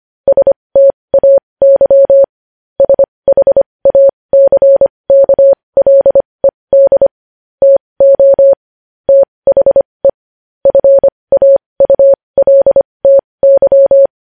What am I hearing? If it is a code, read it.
Morse code, heard as a keyed tone, about 25 wpm, spaced as for 20 wpm, 550 Hz. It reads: STAY SHACKLED TO THE FAULTY
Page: news headlines in morse code at 20 wpm